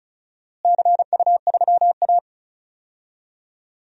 Morse code, heard as a keyed tone, about 36 wpm, 700 Hz. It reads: CU3A
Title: DXCC-100 Call Signs - 4 Characters - Repeated 2x - 35wpm